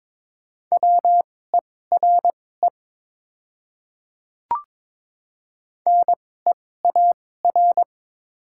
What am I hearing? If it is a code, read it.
Morse code, heard as a keyed tone, about 21 wpm, spaced as for 16 wpm, 700 Hz. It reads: WERE NEAR
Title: 2x Character Spacing - Top 200 Words - 22wpm